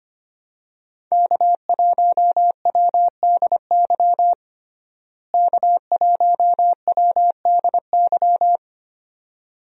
Morse code, heard as a keyed tone, about 25 wpm, 700 Hz. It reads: K1WDY K1WDY
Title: USA Call Signs - 5 Characters - Repeated 2x - 25wpm